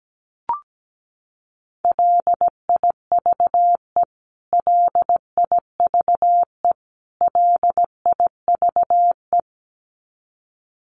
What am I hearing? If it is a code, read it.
Morse code, heard as a keyed tone, about 17 wpm, 700 Hz. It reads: LIVE LIVE LIVE
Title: Top 300 Words Repeated 17wpm